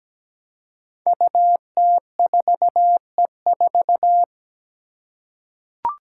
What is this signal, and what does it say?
Morse code, heard as a keyed tone, about 17 wpm, 700 Hz. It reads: UT4E4
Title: Five Challenging Letter-Number Combinations 17wpm